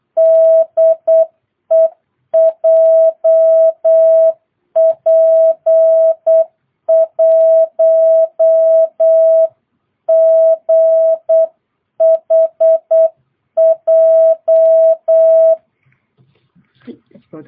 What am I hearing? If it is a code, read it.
Morse code, heard as a keyed tone, about 8 wpm, 660 Hz. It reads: DE JP1GHJ
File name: jh8cbh / 2022cw